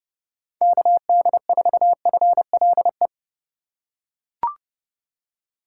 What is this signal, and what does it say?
Morse code, heard as a keyed tone, about 30 wpm, 700 Hz. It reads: KD4FLE
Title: USA Call Signs - 6 Characters - 2x Character Spacing - 30wpm